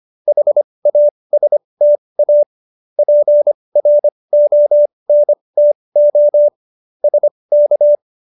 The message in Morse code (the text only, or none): HASTA PRONTO SK